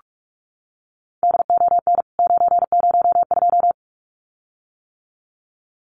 Morse code, heard as a keyed tone, about 44 wpm, 700 Hz. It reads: DON 902
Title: CWT Exchange 45wpm